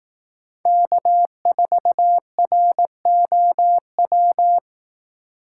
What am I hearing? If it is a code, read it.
Morse code, heard as a keyed tone, about 18 wpm, 700 Hz. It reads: K4ROW
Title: Speed Racing - USA 1x3 Call Signs 15wpm